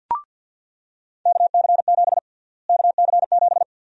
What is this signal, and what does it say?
Morse code, heard as a keyed tone, about 53 wpm, 700 Hz. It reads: KC7 KC7
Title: USA Call Sign Prefixes - Repeated 2x - 50wpm